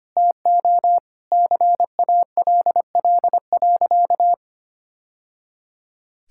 Morse code, heard as a keyed tone, about 25 wpm, 700 Hz. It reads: TO CALL.
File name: practice / 25wpm / Sentences with 3-Word Phrases - 25wpm